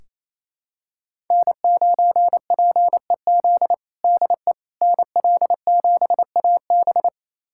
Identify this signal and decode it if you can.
Morse code, heard as a keyed tone, about 28 wpm, 700 Hz. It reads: N9PEZ DE NL7AB
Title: USA Callsign DE Callsign - 5 Characters - 28wpm